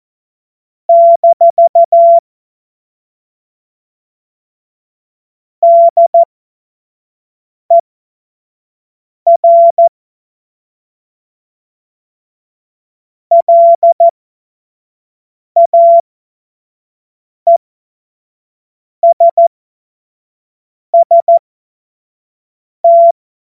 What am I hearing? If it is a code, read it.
Morse code, heard as a keyed tone, about 14 wpm, spaced as for 5 wpm, 680 Hz. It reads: - DER LAESST
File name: Morsekurs / Witze / Witz 28. März 2023 25 F70